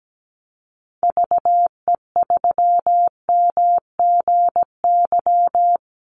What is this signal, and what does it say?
Morse code, heard as a keyed tone, about 17 wpm, 700 Hz. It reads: VE3MGY
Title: Call Signs Repeated 3x - 17wpm